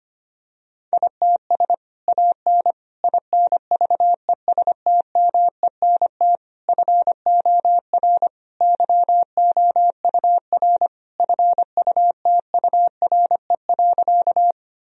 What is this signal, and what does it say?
Morse code, heard as a keyed tone, about 25 wpm, 700 Hz. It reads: ITS AN INVESTMENT FOR YOUR FUTURE.
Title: Sentences from Top 1000 Words - Review Part 4 - 25wpm